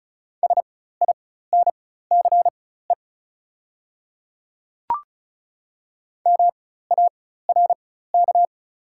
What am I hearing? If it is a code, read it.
Morse code, heard as a keyed tone, about 35 wpm, spaced as for 16 wpm, 700 Hz. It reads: SINCE MARK
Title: 4x Character Spacing - Top 300 Words - 35wpm